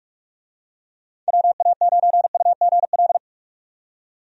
Morse code, heard as a keyed tone, about 45 wpm, 700 Hz. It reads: WA9UGL
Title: USA Call Signs - 6 Characters - 45wpm